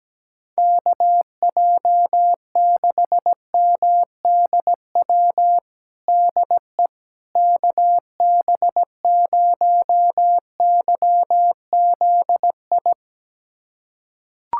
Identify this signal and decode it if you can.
Morse code, heard as a keyed tone, about 17 wpm, 700 Hz. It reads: KJ6MDW DE KB0YZI
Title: USA Callsign DE Callsign 17wpm